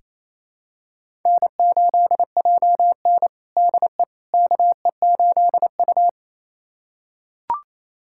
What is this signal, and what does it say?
Morse code, heard as a keyed tone, about 28 wpm, 700 Hz. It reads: N8JN DE KE8U
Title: USA Callsign DE Callsign - 4 Characters - 28wpm